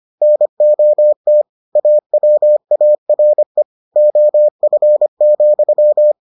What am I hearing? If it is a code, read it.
Morse code, heard as a keyed tone, about 25 wpm, 600 Hz. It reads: NOT AWARE OF,